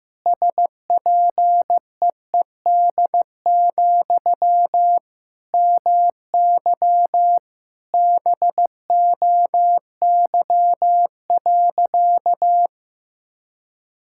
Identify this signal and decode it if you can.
Morse code, heard as a keyed tone, about 15 wpm, 700 Hz. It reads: SPEED, MY BOY.
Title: Sentences from Top 1000 Words - Review Part 5 - 15wpm